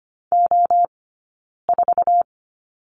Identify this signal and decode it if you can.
Morse code, heard as a keyed tone, about 25 wpm, spaced as for 9 wpm, 690 Hz. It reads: O4
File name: course / 25:10wpm / 092 Lesson 16d - Letter c - Two Characters 25wpm